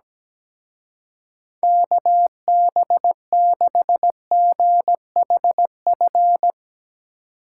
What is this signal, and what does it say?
Morse code, heard as a keyed tone, about 18 wpm, 700 Hz. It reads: KB6GHF
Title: Speed Racing - USA 2x3 Call Signs 17wpm